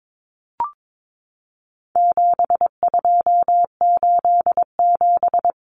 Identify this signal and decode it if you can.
Morse code, heard as a keyed tone, about 22 wpm, 700 Hz. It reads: 7287
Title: Common Cut Numbers - 4-Digit 22wpm